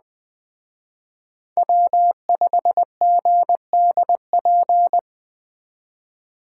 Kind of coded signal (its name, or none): Morse code